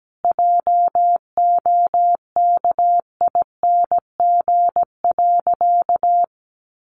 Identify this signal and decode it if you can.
Morse code, heard as a keyed tone, about 17 wpm, 700 Hz. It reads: JOKING.
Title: A Sound Of Thunder 17wpm